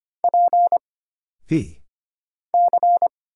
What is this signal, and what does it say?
Morse code, heard as a keyed tone, about 25 wpm, 700 Hz. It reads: P C
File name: Single Character (Letter:Number:Punctuation:Prosign) - Mind-Melt - 25wpm